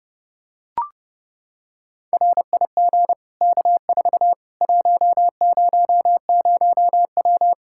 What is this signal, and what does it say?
Morse code, heard as a keyed tone, about 30 wpm, 700 Hz. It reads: RIG K4 100W
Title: Practice QSO - Exchange 2 - Info - 30wpm